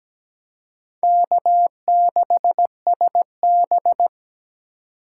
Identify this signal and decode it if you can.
Morse code, heard as a keyed tone, about 17 wpm, 700 Hz. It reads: K6SB
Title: USA Call Signs - 4 Characters - 4x Character Spacing - 17wpm